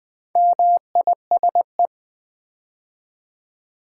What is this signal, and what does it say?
Morse code, heard as a keyed tone, about 20 wpm, 700 Hz. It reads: MISE
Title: Build A Word - 2 Letters At A Time - 6-Letter Words - 20wpm